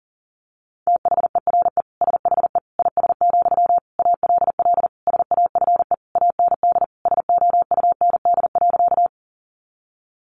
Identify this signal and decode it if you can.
Morse code, heard as a keyed tone, about 37 wpm, 700 Hz. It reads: THERE SHE IS, ALL SAFE AND SOUND.
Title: Sentences from Top 900 Words 40wpm